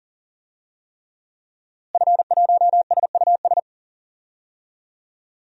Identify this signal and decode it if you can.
Morse code, heard as a keyed tone, about 39 wpm, 700 Hz. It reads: F1SUS
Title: DXCC-100 Call Signs - 5 Characters - Repeated 3x - 40wpm